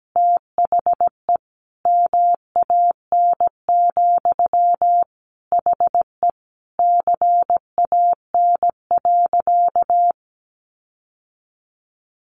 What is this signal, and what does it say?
Morse code, heard as a keyed tone, about 17 wpm, 700 Hz. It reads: THE MAN, HE CAN.
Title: Sentences from Top 1500 Words 17wpm